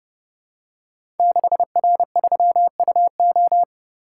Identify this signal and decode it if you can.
Morse code, heard as a keyed tone, about 30 wpm, 700 Hz. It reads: 6R3UO
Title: Code Groups - 5 Letter:Numbers - 30wpm